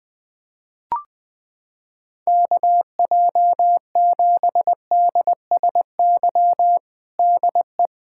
Morse code, heard as a keyed tone, about 19 wpm, 700 Hz. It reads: KJ7DSY DE
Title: K1USN SST QSO Practice 20wpm